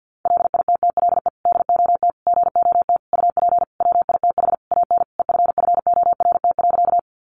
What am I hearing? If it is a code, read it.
Morse code, heard as a keyed tone, about 50 wpm, 700 Hz. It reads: LITTLE DOT GOT UP WITH AN EFFORT.
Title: Sentences from Top 1500 Words 50wpm